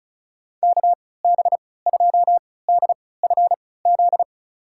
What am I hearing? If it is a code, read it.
Morse code, heard as a keyed tone, about 36 wpm, spaced as for 20 wpm, 700 Hz. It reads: KB2DFZ
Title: USA Call Signs - 3x Character Spacing - 35wpm